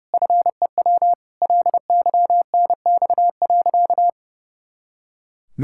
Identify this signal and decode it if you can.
Morse code, heard as a keyed tone, about 30 wpm, 700 Hz. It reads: FEW LYNX.
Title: Pangram Challenge v2 - 30wpm